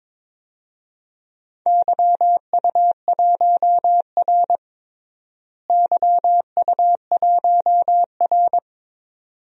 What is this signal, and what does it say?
Morse code, heard as a keyed tone, about 22 wpm, 700 Hz. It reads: YU1R YU1R
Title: DXCC-100 Call Signs - 4 Characters - Repeated 2x - 22wpm